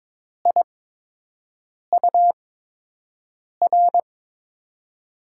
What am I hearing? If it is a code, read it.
Morse code, heard as a keyed tone, about 22 wpm, spaced as for 6 wpm, 700 Hz. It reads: IUR
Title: USA Call Signs - 8x Character Spacing - 22wpm